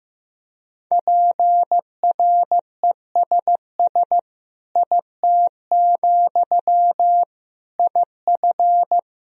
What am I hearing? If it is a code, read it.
Morse code, heard as a keyed tone, about 16 wpm, 700 Hz. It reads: PRESS IT, IF
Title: Sentences from Top 1000 Words - Review Part 1 - 15wpm